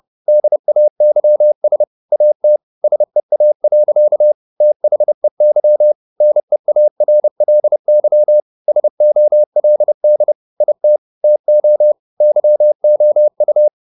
Morse code, heard as a keyed tone, about 30 wpm, 600 Hz. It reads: DAYS AT SEA. THEY NEARLY SOLD IT TO YOU